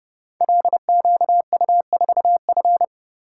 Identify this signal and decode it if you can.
Morse code, heard as a keyed tone, about 31 wpm, 700 Hz. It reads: LQU4F
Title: Code Groups - 5 Letter:Numbers - 30wpm